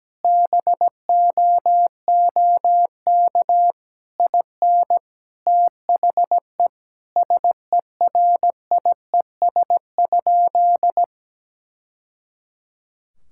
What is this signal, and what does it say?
Morse code, heard as a keyed tone, about 17 wpm, 700 Hz. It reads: BOOK IN THE SERIES?
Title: Sentences from Top 1000 Words - Review Part 5 - 17wpm